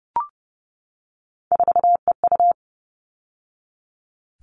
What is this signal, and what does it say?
Morse code, heard as a keyed tone, about 30 wpm, 700 Hz. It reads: 4EU
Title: All Cut Numbers - 3-Digit 30wpm